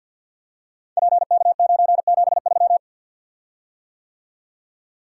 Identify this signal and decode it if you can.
Morse code, heard as a keyed tone, about 46 wpm, 700 Hz. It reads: PK973
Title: Code Groups - 5 Letter:Numbers - 50wpm